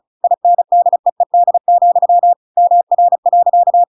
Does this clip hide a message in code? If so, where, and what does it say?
Yes: Morse code from 0.2 to 3.8 s, heard as a keyed tone, about 34 wpm, 700 Hz. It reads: INDEED, MR.